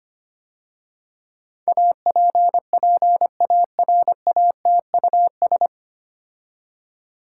Morse code, heard as a keyed tone, about 25 wpm, 700 Hz. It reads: APPARATUS
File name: practice / 25wpm / 9 Letter Words - First 3 Letters Given - 25wpm